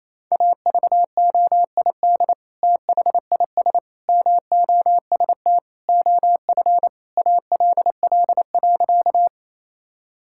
Morse code, heard as a keyed tone, about 27 wpm, 690 Hz. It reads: AVOID THIS MOST OF ALL.
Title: Sentences from Top 1000 Words - Review Part 6 - 28wpm